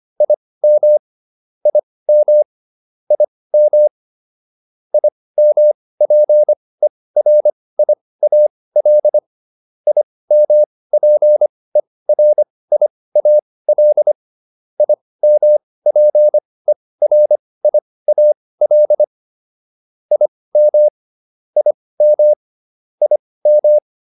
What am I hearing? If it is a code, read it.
Morse code, heard as a keyed tone, about 25 wpm, spaced as for 18 wpm, 600 Hz. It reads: IM IM IM IMPERIAL IMPERIAL IMPERIAL IM IM IM